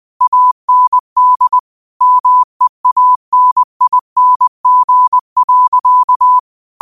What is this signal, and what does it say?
Morse code, heard as a keyed tone, about 20 wpm, 1000 Hz. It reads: AND MEANING.
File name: qotdfiles / 20wpm